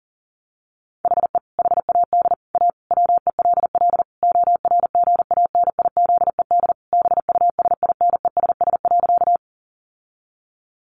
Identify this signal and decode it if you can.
Morse code, heard as a keyed tone, about 40 wpm, 700 Hz. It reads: HE HAD A WELL ORGANIZED BUSINESS.
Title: Sentences from Top 1300 Words 40wpm